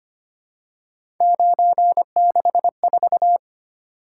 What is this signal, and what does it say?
Morse code, heard as a keyed tone, about 25 wpm, 700 Hz. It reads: 964
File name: Three Letter-Number Combinations - 25wpm